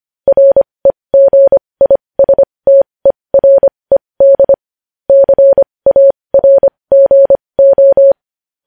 Morse code, heard as a keyed tone, about 25 wpm, spaced as for 20 wpm, 550 Hz. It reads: REGISTERED CARGO